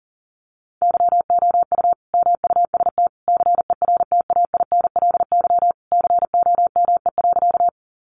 Morse code, heard as a keyed tone, about 40 wpm, 700 Hz. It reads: YOU MUST CERTAINLY COME.
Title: Sentences from Top 900 Words 40wpm